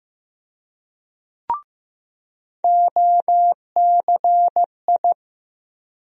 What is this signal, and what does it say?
Morse code, heard as a keyed tone, about 15 wpm, 700 Hz. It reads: OCI
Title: USA Call Sign Postfixes - 3 Letters 15wpm